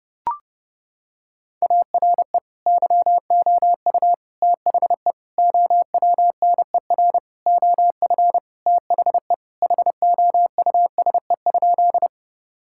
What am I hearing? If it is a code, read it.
Morse code, heard as a keyed tone, about 32 wpm, 700 Hz. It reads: ARE YOU THE OWNER OF THE HOUSE?
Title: Sentences from Top 1600 Words 30wpm